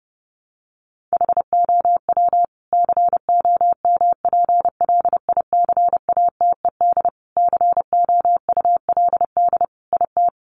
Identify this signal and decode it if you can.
Morse code, heard as a keyed tone, about 28 wpm, 700 Hz. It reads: HOW COMPLICATED COULD IT
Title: Sentences from Top 1200 Words 30wpm